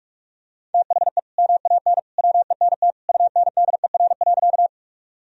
Morse code, heard as a keyed tone, about 41 wpm, 700 Hz. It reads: THE MAN WENT UNDER.
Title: Sentences from Top 1000 Words - Review Part 1 - 45wpm